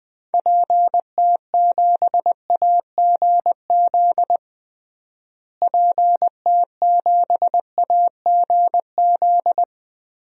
Morse code, heard as a keyed tone, about 21 wpm, 700 Hz. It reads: PT7AGZ PT7AGZ
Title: DXCC-100 Call Signs - 6 Characters - Repeated 3x - 20wpm